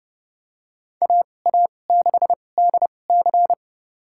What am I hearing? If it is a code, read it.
Morse code, heard as a keyed tone, about 30 wpm, spaced as for 22 wpm, 700 Hz. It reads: AA6DC
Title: USA Call Signs - 5 Characters - 2x Character Spacing - 30wpm